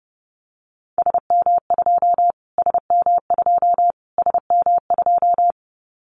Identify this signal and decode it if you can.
Morse code, heard as a keyed tone, about 30 wpm, 690 Hz. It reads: SM2 SM2 SM2
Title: Callsign Prefixes for Top 25 DXCC - Repeated 3x - 30wpm